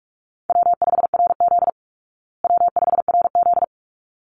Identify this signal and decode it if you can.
Morse code, heard as a keyed tone, about 44 wpm, 700 Hz. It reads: W5RZ W5RZ
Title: Call Signs Repeated 3x - 45wpm